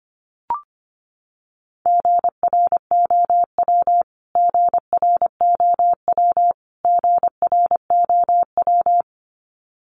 Morse code, heard as a keyed tone, about 25 wpm, 700 Hz. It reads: GROW GROW GROW
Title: Top 500 Words Repeated 25wpm